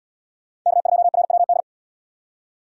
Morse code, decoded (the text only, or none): K2RWD